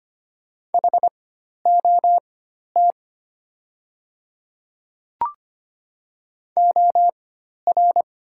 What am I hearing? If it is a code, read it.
Morse code, heard as a keyed tone, about 25 wpm, spaced as for 12 wpm, 700 Hz. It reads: HOT OR
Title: 4x Character Spacing - Top 100 Words - 25wpm